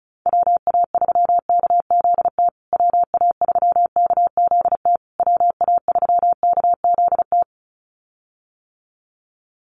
Morse code, heard as a keyed tone, about 34 wpm, 700 Hz. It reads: WA3KZT WA3KZT WA3KZT
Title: USA Call Signs Repeated 3x 35wpm